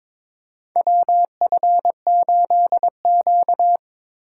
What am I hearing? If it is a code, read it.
Morse code, heard as a keyed tone, about 22 wpm, 700 Hz. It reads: WF8Q
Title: USA Call Signs - 4 Characters - 4x Character Spacing - 22wpm